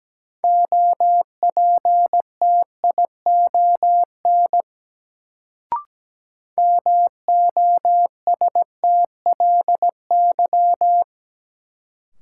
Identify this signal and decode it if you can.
Morse code, heard as a keyed tone, about 17 wpm, 700 Hz. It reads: OPTION MOSTLY
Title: Sentences from Top 1300 Words - New Word Drill Alt - 17wpm